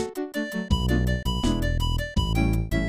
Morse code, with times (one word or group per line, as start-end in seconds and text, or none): none